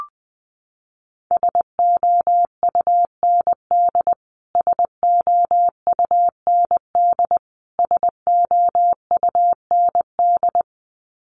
Morse code, heard as a keyed tone, about 19 wpm, 700 Hz. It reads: SOUND SOUND SOUND
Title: Top 500 Words Repeated 20wpm